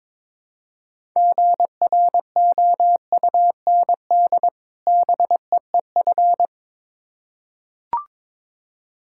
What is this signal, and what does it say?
Morse code, heard as a keyed tone, about 22 wpm, 690 Hz. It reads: GROUND BEEF
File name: Difficult Words with Hints - 22wpm